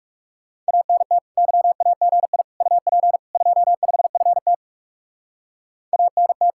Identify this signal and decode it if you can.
Morse code, heard as a keyed tone, about 37 wpm, 700 Hz. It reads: ANT YAGI UP 25FT ANT
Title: Speed-Racing Common QSO Phrases Encore - 30wpm